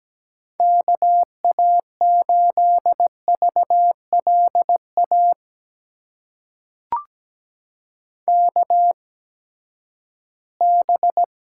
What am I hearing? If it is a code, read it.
Morse code, heard as a keyed tone, about 17 wpm, spaced as for 12 wpm, 700 Hz. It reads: KA8VLA KB
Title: USA Call Signs - 6 Characters - 8x Character Spacing - 17wpm